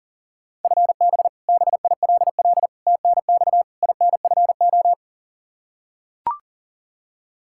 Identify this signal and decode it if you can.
Morse code, heard as a keyed tone, about 40 wpm, 700 Hz. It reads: FB BILL TNX INFO
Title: Practice QSO - Exchange 3 - More Info - 40wpm